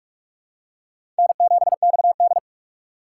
Morse code, heard as a keyed tone, about 46 wpm, 700 Hz. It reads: N7XD
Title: USA Call Signs - 4 Characters - 6x Character Spacing - 45wpm